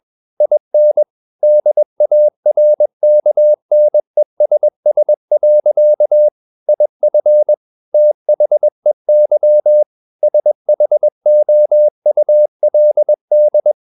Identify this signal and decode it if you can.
Morse code, heard as a keyed tone, about 21 wpm, 600 Hz. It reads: IN DARKNESS. IF THEY SHOULD
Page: CW med Gnister